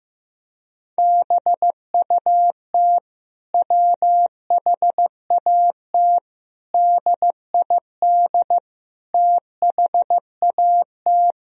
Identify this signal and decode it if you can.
Morse code, heard as a keyed tone, about 15 wpm, 700 Hz. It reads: BUT WHAT DID THAT
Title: Sentences from Top 1000 Words - Review Part 4 - 15wpm